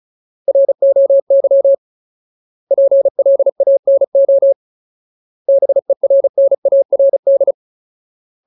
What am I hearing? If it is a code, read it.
Morse code, heard as a keyed tone, about 34 wpm, 550 Hz. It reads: ROY PLANO BERNARD
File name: Ultimate Word List / 35WPM Ultimate